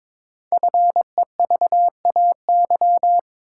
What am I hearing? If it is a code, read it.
Morse code, heard as a keyed tone, about 22 wpm, 700 Hz. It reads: FEVAY
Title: Code Groups - 5 Letter:Numbers - 22wpm